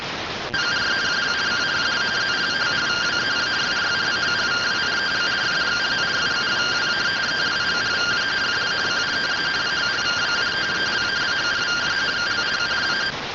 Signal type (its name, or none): two-tone teleprinter signal